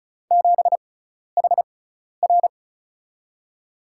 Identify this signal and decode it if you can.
Morse code, heard as a keyed tone, about 36 wpm, spaced as for 12 wpm, 700 Hz. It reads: 7HR
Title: USA Call Signs - 4 Characters - 6x Character Spacing - 35wpm